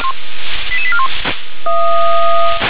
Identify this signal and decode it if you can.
Five-tone selective call (ZVEI), heard as quick runs of five tones, tones of about 80 ms. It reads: 88841